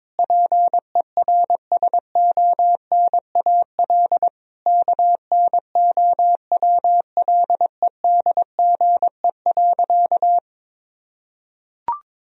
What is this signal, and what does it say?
Morse code, heard as a keyed tone, about 22 wpm, 700 Hz. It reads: PERSONAL KNOWLEDGE.
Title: Sentences from Top 1000 Words - Review Part 4 - 22wpm